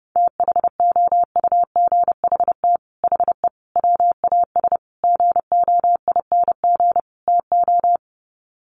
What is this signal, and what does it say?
Morse code, heard as a keyed tone, about 30 wpm, 700 Hz. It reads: THOUGHT HE WAS GOING TO
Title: Sets of 5 Words Encore2 30wpm